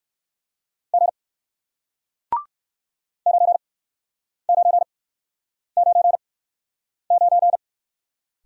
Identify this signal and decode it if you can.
Morse code, heard as a keyed tone, about 58 wpm, spaced as for 9 wpm, 700 Hz. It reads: G 9999